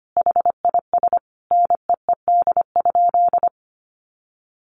Morse code, heard as a keyed tone, about 25 wpm, 700 Hz. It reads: HIS NEED?
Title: Sentences from Top 1100 Words 25wpm